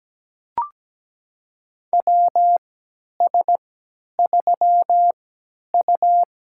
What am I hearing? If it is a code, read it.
Morse code, heard as a keyed tone, about 17 wpm, spaced as for 10 wpm, 700 Hz. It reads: WS3U
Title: USA Call Signs - 4 Characters - 3x Character Spacing - 17wpm